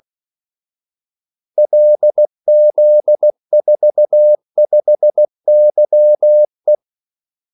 Morse code, heard as a keyed tone, about 16 wpm, 600 Hz. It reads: LZ45YE
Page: Kaldesignaler 16 wpm